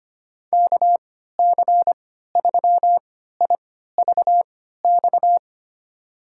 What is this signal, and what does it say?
Morse code, heard as a keyed tone, about 25 wpm, spaced as for 14 wpm, 700 Hz. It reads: KC3IVX
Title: USA Call Signs - 6 Characters - 3x Character Spacing - 25wpm